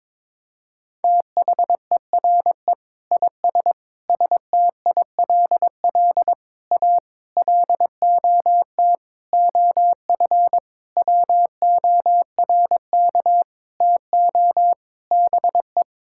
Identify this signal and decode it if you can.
Morse code, heard as a keyed tone, about 22 wpm, 700 Hz. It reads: THERE IS STILL A LOT OF WORK TO BE
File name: Sets of 9 Words - 22wpm